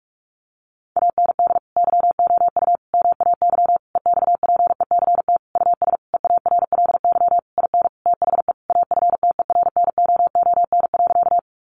Morse code, heard as a keyed tone, about 44 wpm, 700 Hz. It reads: AND YOU MAY EXPECT US EARLY IN THE AFTERNOON.
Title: Sentences from Top 1500 Words 45wpm